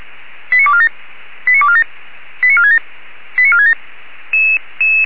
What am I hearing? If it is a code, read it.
Five-tone selective call (ZVEI), heard as quick runs of five tones, tones of about 70 ms. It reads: 89247 89247 89467 89467